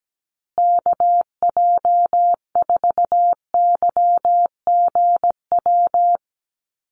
Morse code, heard as a keyed tone, about 17 wpm, 700 Hz. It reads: KJ4YGW
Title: USA Call Signs Repeated 3x 17wpm